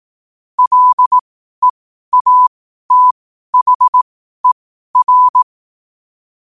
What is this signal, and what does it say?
Morse code, heard as a keyed tone, about 18 wpm, spaced as for 12 wpm, 1000 Hz. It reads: LEATHER
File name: qotdfiles / 12wpm